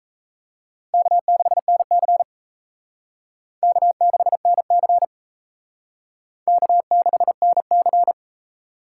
Morse code, heard as a keyed tone, about 36 wpm, 700 Hz. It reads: K6NC K6NC K6NC